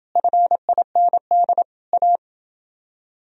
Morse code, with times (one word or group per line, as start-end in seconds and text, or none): 0.2-1.6 FIND
1.9-2.2 A